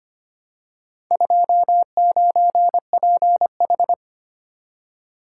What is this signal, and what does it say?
Morse code, heard as a keyed tone, about 25 wpm, 700 Hz. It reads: 29PH
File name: practice / 25wpm / Four Letter-Number Combinations - 25wpm